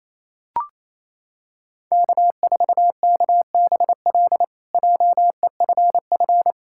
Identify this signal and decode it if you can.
Morse code, heard as a keyed tone, about 28 wpm, 700 Hz. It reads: K4KBL JEFF
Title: K1USN SST QSO Practice - 28wpm